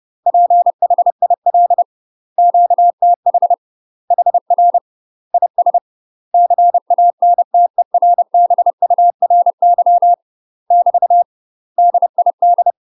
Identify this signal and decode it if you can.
Morse code, heard as a keyed tone, about 30 wpm, 700 Hz. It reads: PHIL QTH HR IS CANTERBURY <BT> DID